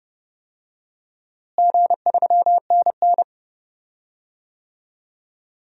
Morse code, heard as a keyed tone, about 30 wpm, 700 Hz. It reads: G3NN
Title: DXCC-100 Call Signs - 4 Characters - Repeated 2x - 30wpm